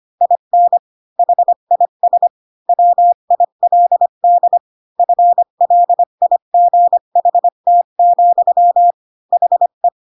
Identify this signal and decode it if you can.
Morse code, heard as a keyed tone, about 25 wpm, 700 Hz. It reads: IN HIS WILD FLIGHT, HE
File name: Pinocchio / Pinocchio ch4 20 25 ik1hge